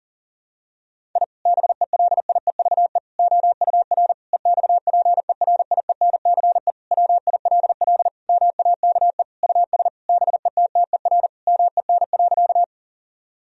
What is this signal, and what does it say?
Morse code, heard as a keyed tone, about 40 wpm, 700 Hz. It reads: I BELIEVE OUR EXPERIENCE WILL MAKE US BETTER MEN.
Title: Sentences from Top 1000 Words - Review Part 3 - 40wpm